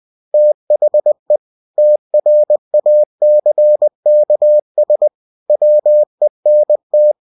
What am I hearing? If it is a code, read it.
Morse code, heard as a keyed tone, about 20 wpm, 600 Hz. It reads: THE TRACKS WENT